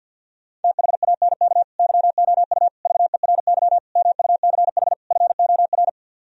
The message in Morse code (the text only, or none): THANK YOU VERY MUCH FOR